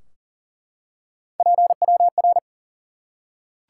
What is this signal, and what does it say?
Morse code, heard as a keyed tone, about 40 wpm, 700 Hz. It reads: PWR
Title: Top 90 QSO Elements 40wpm